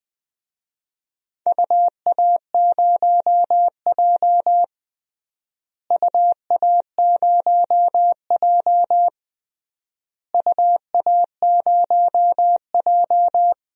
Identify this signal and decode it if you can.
Morse code, heard as a keyed tone, about 20 wpm, 700 Hz. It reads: UA0J UA0J UA0J
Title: DXCC-100 Call Signs - 4 Characters - Repeated 3x - 20wpm